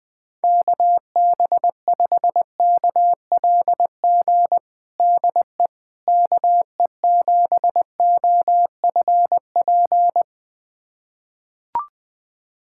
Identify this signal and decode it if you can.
Morse code, heard as a keyed tone, about 20 wpm, 700 Hz. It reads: KB5KLG DE KE7OFP
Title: USA Callsign DE Callsign 20wpm